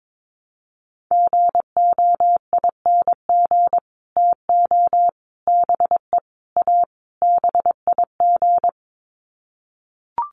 Morse code, heard as a keyed tone, about 22 wpm, 700 Hz. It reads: GOING TO BE A BIG
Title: Sets of 5 Words Encore2 22wpm